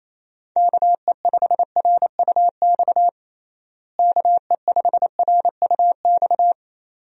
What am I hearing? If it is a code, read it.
Morse code, heard as a keyed tone, about 28 wpm, 700 Hz. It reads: KE5RUX KE5RUX
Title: USA Call Signs - 6 Characters - Repeated 2x - 28wpm